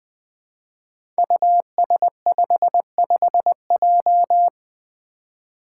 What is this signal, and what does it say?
Morse code, heard as a keyed tone, about 20 wpm, 700 Hz. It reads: US55J
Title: Five Challenging Letter-Number Combinations 20wpm